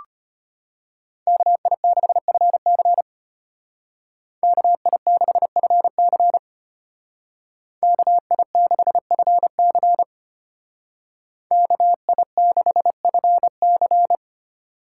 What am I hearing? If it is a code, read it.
Morse code, heard as a keyed tone, about 30 wpm, 700 Hz. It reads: KI6FC KI6FC KI6FC KI6FC
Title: Speed Racing - USA 2x2 Call Signs 25wpm